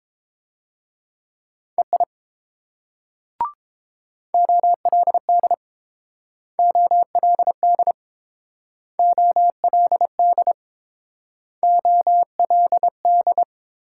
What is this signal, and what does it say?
Morse code, heard as a keyed tone, about 30 wpm, 700 Hz. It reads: EI OLD OLD OLD OLD